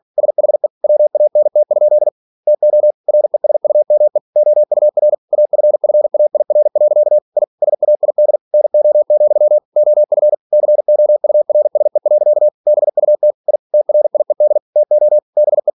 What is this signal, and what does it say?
Morse code, heard as a keyed tone, about 46 wpm, 600 Hz. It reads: SHE WANT? TO RESUME OUR AFFAIR. I SAID NO, OF COURSE. BUT I TRIED TO BE
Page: Never 47wpm